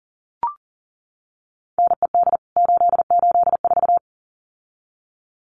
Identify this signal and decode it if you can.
Morse code, heard as a keyed tone, about 40 wpm, 700 Hz. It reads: NED 884
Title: CWT Exchange 40wpm